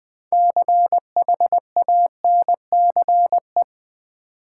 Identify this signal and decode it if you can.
Morse code, heard as a keyed tone, about 21 wpm, 700 Hz. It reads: CHANCE
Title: Sets of 3 Words - New Word Drill - 20wpm